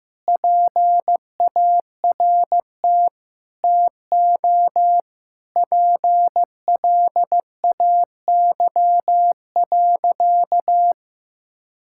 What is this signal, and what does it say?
Morse code, heard as a keyed tone, about 15 wpm, 700 Hz. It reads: PART TO PLAY.
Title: Sentences from Top 1000 Words - Review Part 6 - 15wpm